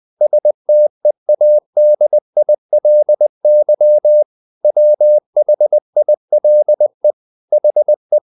Morse code, heard as a keyed tone, about 20 wpm, 600 Hz. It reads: STEADILY WHILE HE
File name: the old man and the sea part 2 20wpm